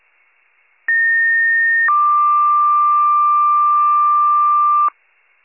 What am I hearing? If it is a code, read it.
Two-tone page, heard as a one-second tone, then a three-second tone, about 1850 then 1180 Hz. It is heard from 0.9 to 4.9 s.